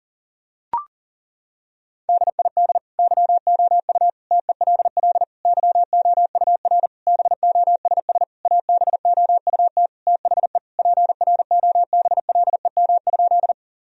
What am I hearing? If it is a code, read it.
Morse code, heard as a keyed tone, about 40 wpm, 700 Hz. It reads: DID YOU TELL YOUR BOSS ABOUT THE PROBLEM?
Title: Sentences from Top 1000 Words - Review Part 4 - 40wpm